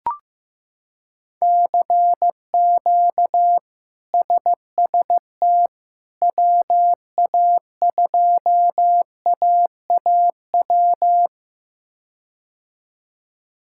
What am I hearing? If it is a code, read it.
Morse code, heard as a keyed tone, about 15 wpm, 700 Hz. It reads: CQ SST WA2AAW
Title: K1USN SST QSO Practice - 15wpm